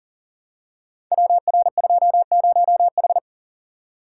Morse code, heard as a keyed tone, about 40 wpm, 700 Hz. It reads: WR20H
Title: Code Groups - 5 Letter:Numbers - 40wpm